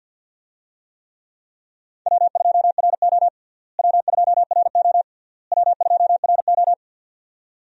Morse code, heard as a keyed tone, about 47 wpm, 700 Hz. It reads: W2RO W2RO W2RO
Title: USA Call Signs - 4 Characters - Repeated 3x - 50wpm